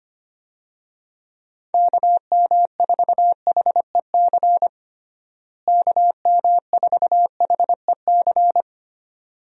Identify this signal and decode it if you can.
Morse code, heard as a keyed tone, about 25 wpm, 700 Hz. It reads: KM4HEC KM4HEC
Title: USA Call Signs - 6 Characters - Repeated 2x - 25wpm